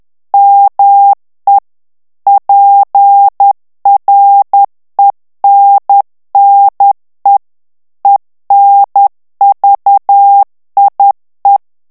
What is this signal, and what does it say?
Morse code, heard as a keyed tone, about 10 wpm, 800 Hz. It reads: ME PRENNE ENVIE
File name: morse mp3 / FRANCAIS TEXTE CLAIR 2 VITESSE 10 MOTS MINUTES